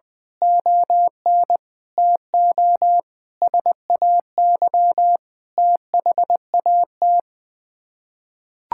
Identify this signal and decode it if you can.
Morse code, heard as a keyed tone, about 20 wpm, 700 Hz. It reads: ON TO SAY THAT
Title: Sets of 6 Words - 20wpm